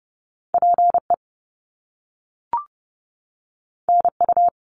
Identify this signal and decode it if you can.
Morse code, heard as a keyed tone, about 30 wpm, 700 Hz. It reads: PE NU
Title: Canadian Provinces and Territories 30wpm